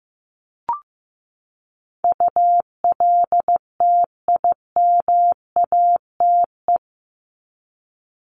Morse code, heard as a keyed tone, about 15 wpm, 700 Hz. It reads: ULTIMATE
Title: LY Suffix - Top 100 Words with Base Words 15wpm